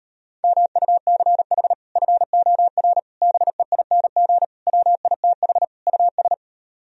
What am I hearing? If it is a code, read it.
Morse code, heard as a keyed tone, about 38 wpm, 700 Hz. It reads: MUCH FOR BEING WITH US